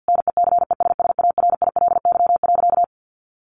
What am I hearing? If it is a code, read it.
Morse code, heard as a keyed tone, about 50 wpm, 700 Hz. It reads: NECESSARILY.
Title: Sentences from Top 1400 Words 50wpm